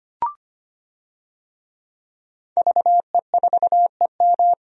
Morse code, heard as a keyed tone, about 25 wpm, 700 Hz. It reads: VE4EM
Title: DXCC-100 Call Signs - 5 Characters - 25wpm